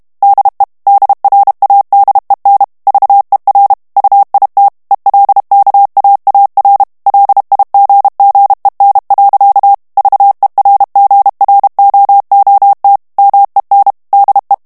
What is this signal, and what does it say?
Morse code, heard as a keyed tone, about 32 wpm, 800 Hz. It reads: DE DRADEN VER UIT ELKAAR LIGGEN. VERGROOT MEN DE
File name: morse mp3 / HOLLANDAIS texte clair 1 vitesse 30 mots minute